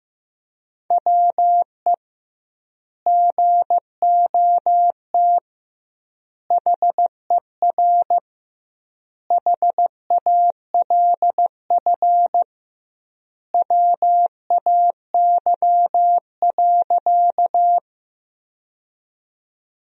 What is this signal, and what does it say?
Morse code, heard as a keyed tone, about 14 wpm, 700 Hz. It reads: WE GOT HER HALF WAY.
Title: 2x Word-Spacing - Sentences from Top 400 Words - 15wpm